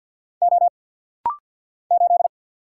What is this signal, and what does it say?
Morse code, heard as a keyed tone, about 50 wpm, 700 Hz. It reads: O 8
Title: Single Letter-Number - Rapid-Fire 50wpm